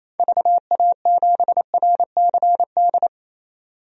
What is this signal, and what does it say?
Morse code, heard as a keyed tone, about 28 wpm, 700 Hz. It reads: VA7RCD
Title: DXCC-100 Call Signs - 6 Characters - 28wpm